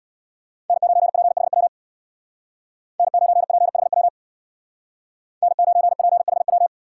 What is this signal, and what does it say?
Morse code, heard as a keyed tone, about 64 wpm, 700 Hz. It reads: N9PHW N9PHW N9PHW